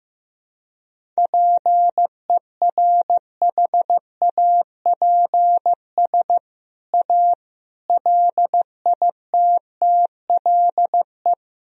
Morse code, heard as a keyed tone, about 15 wpm, 700 Hz. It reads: PERHAPS A LITTLE
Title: Sentences from Top 1000 Words - Review Part 1 - 15wpm